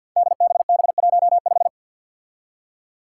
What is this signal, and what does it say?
Morse code, heard as a keyed tone, about 53 wpm, 700 Hz. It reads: DBB15
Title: Five Challenging Letter-Number Combinations 50wpm